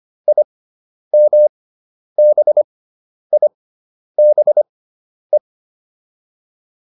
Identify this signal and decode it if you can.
Morse code, heard as a keyed tone, about 25 wpm, spaced as for 10 wpm, 600 Hz. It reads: IMBIBE
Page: intermediate-practice-files